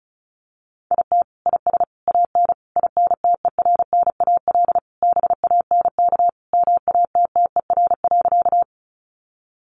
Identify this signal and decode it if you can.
Morse code, heard as a keyed tone, about 34 wpm, 700 Hz. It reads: IT IS AN INTERNAL BANK MATTER.
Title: Sentences from Top 1400 Words 35wpm